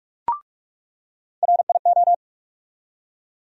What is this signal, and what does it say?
Morse code, heard as a keyed tone, about 42 wpm, 700 Hz. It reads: RIO
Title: USA Call Sign Postfixes - 3 Letters 45wpm